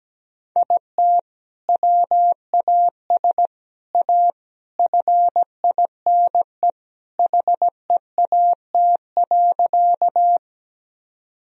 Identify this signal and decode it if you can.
Morse code, heard as a keyed tone, about 17 wpm, 700 Hz. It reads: IT WAS A FINE HEAT.
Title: Sentences from Top 1000 Words - Review Part 3 - 17wpm